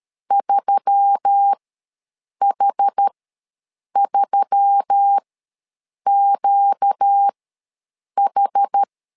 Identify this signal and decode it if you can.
Morse code, heard as a keyed tone, about 13 wpm, spaced as for 7 wpm, 800 Hz. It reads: 3H3QH
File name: LEZIONI / 4 Q1S3 ws 12wpm 2038